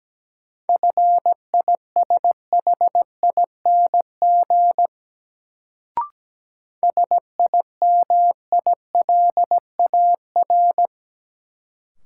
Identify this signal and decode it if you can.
Morse code, heard as a keyed tone, about 17 wpm, 700 Hz. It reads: FISHING SIMILAR
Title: Sentences from Top 1000 Words - Review Part 5 - New Word Drill - 17wpm